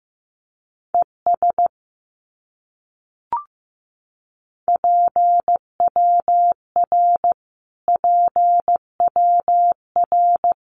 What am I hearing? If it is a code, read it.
Morse code, heard as a keyed tone, about 15 wpm, 700 Hz. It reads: ES PWR PWR
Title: Abbreviations Repeated 3x - 15wpm